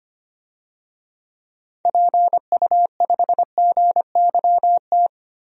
Morse code, heard as a keyed tone, about 25 wpm, 700 Hz. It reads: PU5GYT
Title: DXCC-100 Call Signs - 6 Characters - Repeated 2x - 25wpm